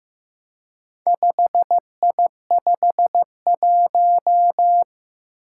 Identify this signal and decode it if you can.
Morse code, heard as a keyed tone, about 15 wpm, 700 Hz. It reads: 5I51
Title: Four Challenging Letter-Number Combinations 15wpm